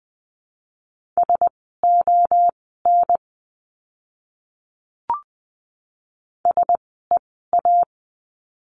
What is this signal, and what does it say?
Morse code, heard as a keyed tone, about 20 wpm, spaced as for 14 wpm, 700 Hz. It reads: SON SEA
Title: 061 Lesson 11f - Letter h - Words 20wpm